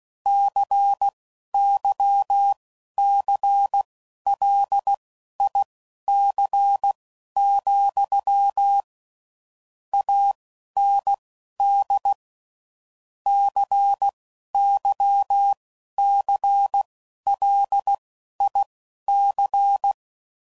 Morse code, heard as a keyed tone, about 16 wpm, spaced as for 11 wpm, 800 Hz. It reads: CYCLIC, AND CYCLIC
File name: CW Code Groups @10WPM / Technical Subjects / XC